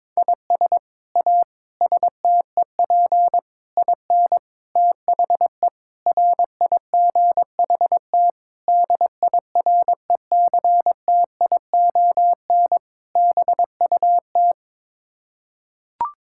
Morse code, heard as a keyed tone, about 22 wpm, 700 Hz. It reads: IS A STEP IN THE RIGHT DIRECTION BUT
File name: practice / 22wpm / Sets of 8 Words - 22wpm